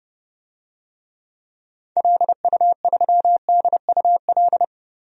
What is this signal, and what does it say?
Morse code, heard as a keyed tone, about 30 wpm, 700 Hz. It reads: LU3DUL
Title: DXCC-100 Call Signs - 6 Characters - 30wpm